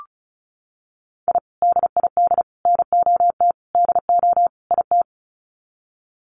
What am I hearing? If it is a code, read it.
Morse code, heard as a keyed tone, about 34 wpm, 700 Hz. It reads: I DID NOT DO IT
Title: Sets of 5 Words Encore2 35wpm